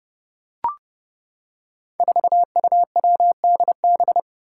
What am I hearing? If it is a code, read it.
Morse code, heard as a keyed tone, about 30 wpm, 700 Hz. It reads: 4UWDB
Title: Five Challenging Letter-Number Combinations 30wpm